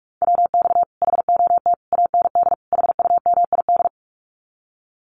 Morse code, heard as a keyed tone, about 44 wpm, 700 Hz. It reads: WX HOT AND HUMID
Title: Common QSO Phrases Encore 45wpm